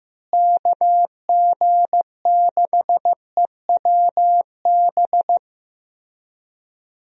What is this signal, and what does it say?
Morse code, heard as a keyed tone, about 15 wpm, 700 Hz. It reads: KG6EWB
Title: USA Call Signs - 6 Characters - Repeated 3x - 15wpm